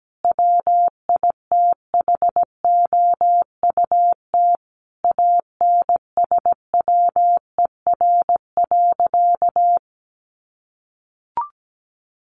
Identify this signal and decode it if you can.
Morse code, heard as a keyed tone, about 17 wpm, 700 Hz. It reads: WITHOUT ANSWER.
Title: Sentences from Top 1500 Words 17wpm